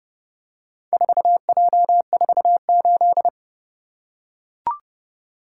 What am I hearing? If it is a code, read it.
Morse code, heard as a keyed tone, about 29 wpm, 700 Hz. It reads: 4J48